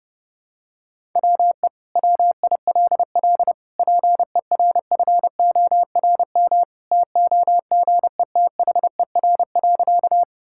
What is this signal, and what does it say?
Morse code, heard as a keyed tone, about 30 wpm, 700 Hz. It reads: WE WILL PERFORM TOGETHER.
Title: Sentences from Top 1000 Words - Review Part 5 - 30wpm